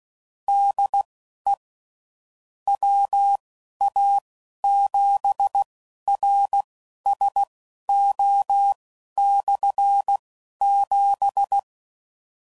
Morse code, heard as a keyed tone, about 16 wpm, spaced as for 11 wpm, 800 Hz. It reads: DE WA7RSO/7
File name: Basic Sets / QST de WA7RSO / Combnations